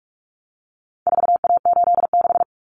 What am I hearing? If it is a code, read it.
Morse code, heard as a keyed tone, about 44 wpm, 700 Hz. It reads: 4A86
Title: Common Cut Numbers - 4-Digit 45wpm